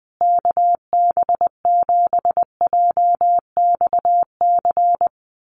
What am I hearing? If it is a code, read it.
Morse code, heard as a keyed tone, about 20 wpm, 700 Hz. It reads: KB7JXC
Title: USA Call Signs Repeated 3x 20wpm